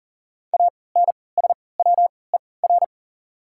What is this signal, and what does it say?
Morse code, heard as a keyed tone, about 40 wpm, spaced as for 23 wpm, 700 Hz. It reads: ANSWER
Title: 3x Character Spacing - Top 300 Words - 40wpm